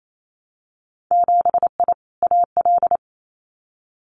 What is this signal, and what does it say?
Morse code, heard as a keyed tone, about 28 wpm, 700 Hz. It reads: 7I AL
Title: Winter Field Day Exchanges Encore 28wpm